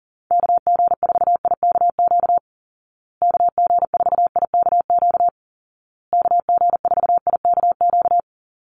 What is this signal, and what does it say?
Morse code, heard as a keyed tone, about 40 wpm, 700 Hz. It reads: KG4IKQ KG4IKQ KG4IKQ
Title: Call Signs Repeated 2x - 40wpm